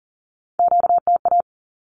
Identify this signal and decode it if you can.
Morse code, heard as a keyed tone, about 40 wpm, 700 Hz. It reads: QTA
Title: Q-Codes 40wpm